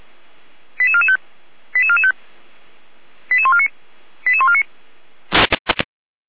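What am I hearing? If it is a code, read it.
Five-tone selective call (ZVEI), heard as quick runs of five tones, tones of about 70 ms. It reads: 88445 88445 88149 88149
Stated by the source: ZVEI five-tone selective call received by SDR, fire brigade Feuerwehr Orlamünde.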